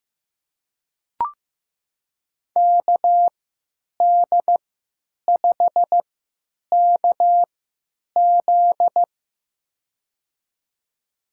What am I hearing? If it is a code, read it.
Morse code, heard as a keyed tone, about 15 wpm, spaced as for 9 wpm, 700 Hz. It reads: KD5KZ
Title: USA Call Signs - 5 Characters - 3x Character Spacing - 15wpm